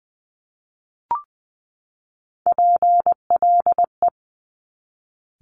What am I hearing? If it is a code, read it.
Morse code, heard as a keyed tone, about 20 wpm, 700 Hz. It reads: PLE
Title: Top Three Letter Combinations in Words 20wpm